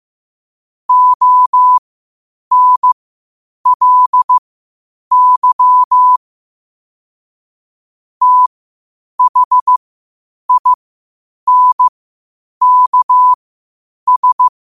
Morse code, heard as a keyed tone, about 15 wpm, spaced as for 8 wpm, 1000 Hz. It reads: ONLY THINKS